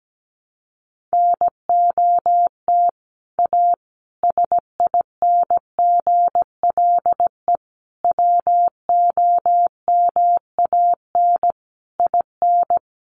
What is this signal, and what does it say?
Morse code, heard as a keyed tone, about 17 wpm, 700 Hz. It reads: NOT A SINGLE WOMAN IN
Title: Sentences from Top 1100 Words 17wpm